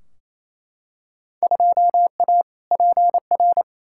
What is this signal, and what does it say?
Morse code, heard as a keyed tone, about 28 wpm, 700 Hz. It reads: 2A PR
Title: Field Day Exchanges Encore - 28wpm